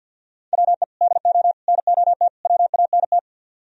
Morse code, heard as a keyed tone, about 61 wpm, 700 Hz. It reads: WE DO NOT WANT
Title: Speed-Racing - Sets of 4 Words 50wpm